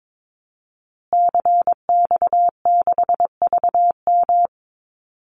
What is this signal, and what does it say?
Morse code, heard as a keyed tone, about 22 wpm, 700 Hz. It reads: CX6VM
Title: Call Signs Repeated 2x - 22wpm